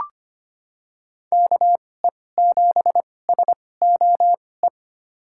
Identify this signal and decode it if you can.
Morse code, heard as a keyed tone, about 25 wpm, spaced as for 18 wpm, 700 Hz. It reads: KE7SOE
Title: USA Call Signs - 2x Character Spacing - 25wpm